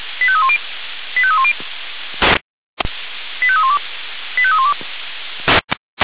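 Five-tone selective call (ZVEI), heard as quick runs of five tones, tones of about 70 ms. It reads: 85310 85310 85312 85312
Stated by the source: ZVEI five-tone selective call received on an SDR.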